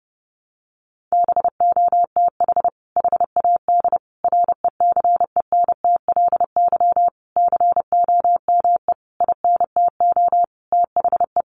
Morse code, heard as a keyed tone, about 30 wpm, 700 Hz. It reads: BOTH HAD RECENTLY COME INTO THE
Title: Sentences from Top 1200 Words 30wpm